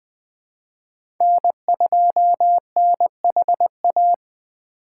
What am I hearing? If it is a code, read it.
Morse code, heard as a keyed tone, about 20 wpm, 700 Hz. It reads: N2NHA
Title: USA Call Signs - 5 Characters - 4x Character Spacing - 20wpm